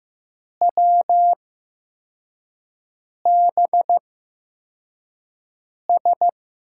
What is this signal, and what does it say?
Morse code, heard as a keyed tone, about 15 wpm, spaced as for 4 wpm, 700 Hz. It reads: WBS
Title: USA Call Signs - 6 Characters - 8x Character Spacing - 15wpm